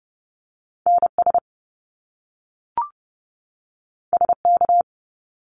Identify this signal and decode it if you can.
Morse code, heard as a keyed tone, about 30 wpm, 700 Hz. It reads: NS SK
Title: Canadian Provinces and Territories 30wpm